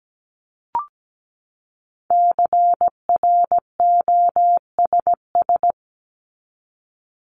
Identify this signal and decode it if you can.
Morse code, heard as a keyed tone, about 17 wpm, 700 Hz. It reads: CROSS
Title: Top 300 Words 17wpm